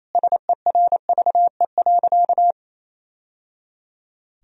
Morse code, heard as a keyed tone, about 28 wpm, 700 Hz. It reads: SERVE.
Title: Sentences from Top 1000 Words - Review Part 4 - 28wpm